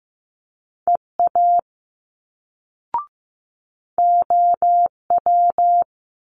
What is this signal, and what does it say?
Morse code, heard as a keyed tone, about 15 wpm, 700 Hz. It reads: EA OW
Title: Multi-letter Phonemes Quick Fire 15wpm